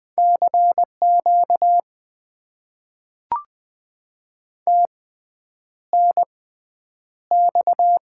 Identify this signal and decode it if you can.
Morse code, heard as a keyed tone, about 20 wpm, spaced as for 9 wpm, 700 Hz. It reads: CQ TNX
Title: Top 90 QSO Elements - 6x Character Spacing - 20wpm